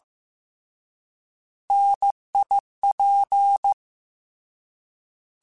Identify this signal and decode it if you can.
Morse code, heard as a keyed tone, about 15 wpm, 800 Hz. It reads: NIP
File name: Alphabetical / Alphabetical mp3s / Ns @13